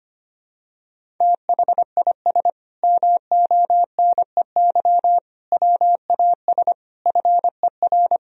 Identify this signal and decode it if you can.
Morse code, heard as a keyed tone, about 25 wpm, 700 Hz. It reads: THIS MONEY WAS FER
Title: Sentences from Top 400 Words and Abbreviations 25wpm